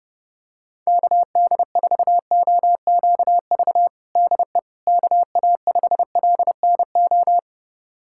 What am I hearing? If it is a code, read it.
Morse code, heard as a keyed tone, about 32 wpm, 700 Hz. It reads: KD4OQV DE KA5LNO
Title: USA Callsign DE Callsign - 6 Characters - 30wpm